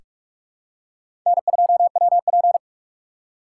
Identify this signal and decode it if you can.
Morse code, heard as a keyed tone, about 45 wpm, 700 Hz. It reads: N2WP
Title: USA Call Signs - 4x Character Spacing - 45wpm